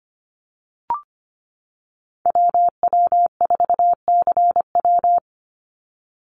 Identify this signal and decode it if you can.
Morse code, heard as a keyed tone, about 25 wpm, 700 Hz. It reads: WW4CW
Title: USA Call Signs 25wpm